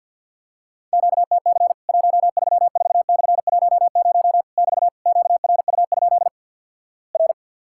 Morse code, heard as a keyed tone, about 50 wpm, 700 Hz. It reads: QTC 134/10 <BT> QRU? R
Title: Worked All Europe DX Contest - QTC Exchange with Repeats - 50wpm